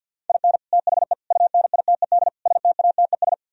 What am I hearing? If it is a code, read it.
Morse code, heard as a keyed tone, about 64 wpm, spaced as for 57 wpm, 700 Hz. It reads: IN THE UNITED STATES
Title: Speed-Racing - Sets of 4 Words 50wpm